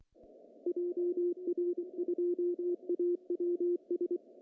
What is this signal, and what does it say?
Morse code, heard as a keyed tone, about 24 wpm, 350 Hz, with radio noise. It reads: JR2AWS
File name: CW / cqww-cw-k5tr